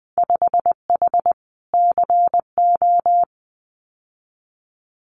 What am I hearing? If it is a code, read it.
Morse code, heard as a keyed tone, about 20 wpm, 700 Hz. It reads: 5H CO
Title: Winter Field Day Exchanges Encore 20wpm